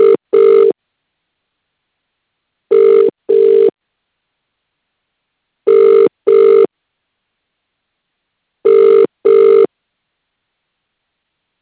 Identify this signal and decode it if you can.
Ringback tone (UK), 400 + 450 Hz.